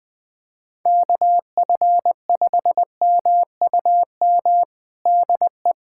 Morse code, heard as a keyed tone, about 20 wpm, 700 Hz. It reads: KF5MUM DE
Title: USA Callsign DE Callsign - 6 Characters - 20wpm